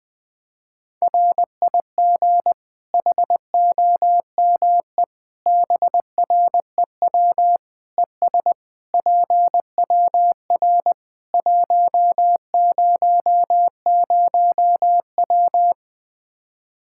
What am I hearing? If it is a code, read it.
Morse code, heard as a keyed tone, about 20 wpm, 700 Hz. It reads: RIG HOME BREW ES PWR 100W
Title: Practice QSO - Exchange 2 - Info - 20wpm